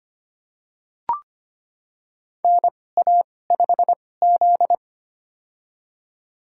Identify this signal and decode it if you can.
Morse code, heard as a keyed tone, about 25 wpm, spaced as for 18 wpm, 700 Hz. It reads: NA5Z
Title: USA Call Signs - 4 Characters - 2x Character Spacing - 25wpm